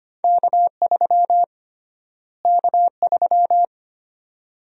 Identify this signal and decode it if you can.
Morse code, heard as a keyed tone, about 25 wpm, 700 Hz. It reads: K3 K3
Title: USA Call Sign Prefixes - Repeated 2x - 25wpm